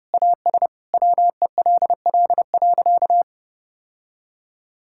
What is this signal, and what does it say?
Morse code, heard as a keyed tone, about 30 wpm, 700 Hz. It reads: AS WELL.
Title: Sentences from Top 1000 Words - Review Part 5 - 30wpm